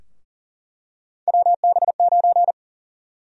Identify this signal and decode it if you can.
Morse code, heard as a keyed tone, about 40 wpm, 700 Hz. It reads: WB9
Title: USA Call Sign Prefixes - 40wpm